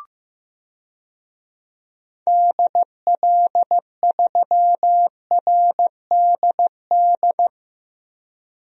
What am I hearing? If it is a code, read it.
Morse code, heard as a keyed tone, about 15 wpm, 700 Hz. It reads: DL3RDD
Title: DXCC-100 Call Signs - 6 Characters - Repeated 3x - 15wpm